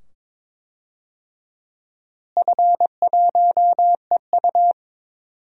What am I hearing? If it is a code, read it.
Morse code, heard as a keyed tone, about 21 wpm, 700 Hz. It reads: F1EU
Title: DXCC-100 Call Signs - 4 Characters - Repeated 2x - 22wpm